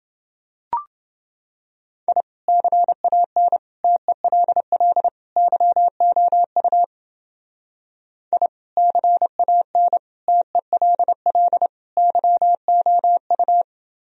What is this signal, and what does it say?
Morse code, heard as a keyed tone, about 28 wpm, 700 Hz. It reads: I CAN TELL YOU I CAN TELL YOU